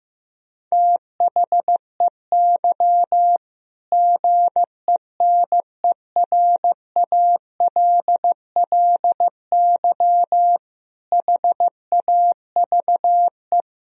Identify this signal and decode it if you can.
Morse code, heard as a keyed tone, about 15 wpm, 700 Hz. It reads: THEY GENERALLY HAVE
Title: Sentences from Top 1000 Words - Review Part 4 - 15wpm